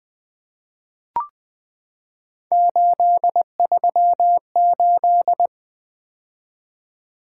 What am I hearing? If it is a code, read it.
Morse code, heard as a keyed tone, about 20 wpm, 700 Hz. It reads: 838
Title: Three Letter-Number Combinations - 20wpm